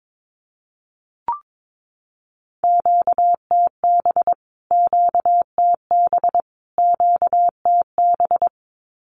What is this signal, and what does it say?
Morse code, heard as a keyed tone, about 22 wpm, 700 Hz. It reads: QTB QTB QTB
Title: Q-Codes Repeated 3x 22wpm